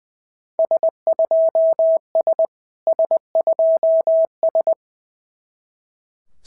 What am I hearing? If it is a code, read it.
Morse code, heard as a keyed tone, about 18 wpm, 650 Hz. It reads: S2S S2S
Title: SOTA QSO Practice 20wpm